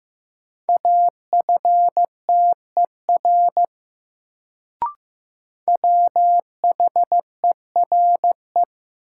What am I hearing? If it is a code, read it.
Morse code, heard as a keyed tone, about 15 wpm, 700 Hz. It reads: AFTER WHERE
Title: Sets of 2 Words - New Word Drill - 15wpm